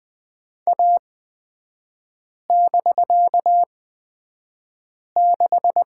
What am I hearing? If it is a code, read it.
Morse code, heard as a keyed tone, about 20 wpm, spaced as for 5 wpm, 700 Hz. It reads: A<BK>6
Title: 250 Lesson 40e - BK - Three Characters 20wpm